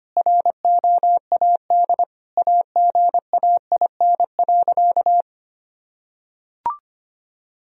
Morse code, heard as a keyed tone, about 25 wpm, 700 Hz. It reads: ROAD AGAIN.
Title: Sentences from Top 1000 Words - Review Part 2 - 25wpm